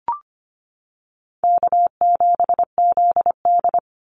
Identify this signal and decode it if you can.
Morse code, heard as a keyed tone, about 25 wpm, 700 Hz. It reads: K7ZD
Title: Easy Call Signs - 25wpm